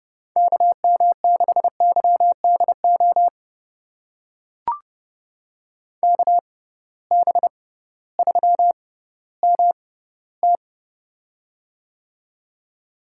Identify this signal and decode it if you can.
Morse code, heard as a keyed tone, about 30 wpm, spaced as for 16 wpm, 700 Hz. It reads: KM6YDO KB3MT
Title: USA Call Signs - 6x Character Spacing - 30wpm